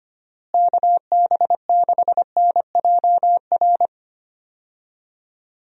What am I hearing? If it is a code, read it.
Morse code, heard as a keyed tone, about 25 wpm, 700 Hz. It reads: KB6NJR
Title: USA Call Signs - 6 Characters - Repeated 3x - 25wpm